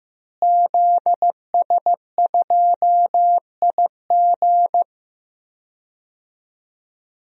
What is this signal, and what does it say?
Morse code, heard as a keyed tone, about 16 wpm, 700 Hz. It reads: ZS2IG
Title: DXCC-100 Call Signs - 5 Characters - Repeated 3x - 15wpm